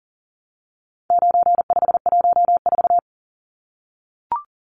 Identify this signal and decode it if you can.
Morse code, heard as a keyed tone, about 40 wpm, 700 Hz. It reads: 9514
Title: Numbers - 4-Digit 40wpm